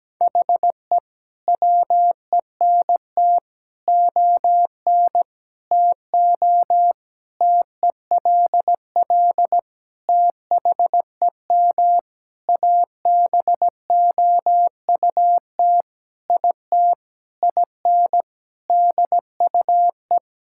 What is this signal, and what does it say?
Morse code, heard as a keyed tone, about 17 wpm, 700 Hz. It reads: HE WENT ON TO TELL THEM ABOUT IT IN DUE
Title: Sentences from Top 1000 Words - Review Part 6 - 17wpm